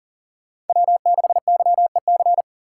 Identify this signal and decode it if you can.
Morse code, heard as a keyed tone, about 50 wpm, 700 Hz. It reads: W6YEC